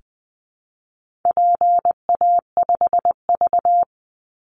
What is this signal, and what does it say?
Morse code, heard as a keyed tone, about 20 wpm, 700 Hz. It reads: PA5V
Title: Call Signs Repeated 3x - 20wpm